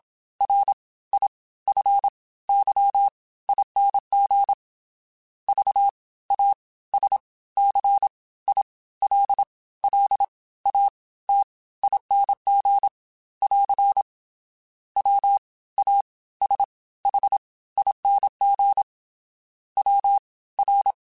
Morse code, heard as a keyed tone, about 27 wpm, spaced as for 17 wpm, 800 Hz. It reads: RIFYING VASCILLATING <AR> WASHING WR